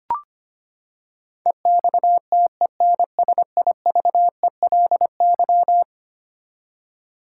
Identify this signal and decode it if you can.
Morse code, heard as a keyed tone, about 25 wpm, 700 Hz. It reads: EXTENSIVELY
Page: morse-code-ninja-practice-25wpm